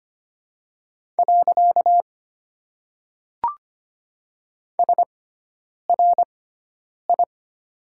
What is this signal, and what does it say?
Morse code, heard as a keyed tone, about 25 wpm, 700 Hz. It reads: . S R I
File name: Top 90 QSO Elements - 6x Character Spacing - 25wpm